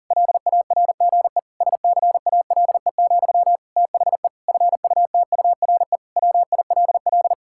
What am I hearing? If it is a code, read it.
Morse code, heard as a keyed tone, about 37 wpm, 700 Hz. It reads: LARGE SCALE, THE FUTURE WILL
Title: Sentences from Top 1000 Words - Review Part 5 - 40wpm